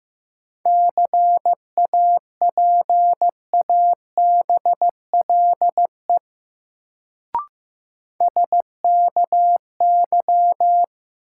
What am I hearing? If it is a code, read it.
Morse code, heard as a keyed tone, about 15 wpm, 700 Hz. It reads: CAPABLE SKY
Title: Sentences from Top 1200 Words - New Word Drill Alt - 15wpm